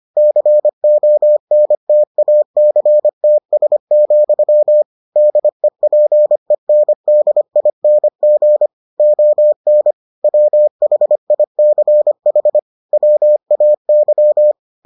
Morse code, heard as a keyed tone, about 25 wpm, 600 Hz. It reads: CONTACTS, DEPENDING ON WHICH WAY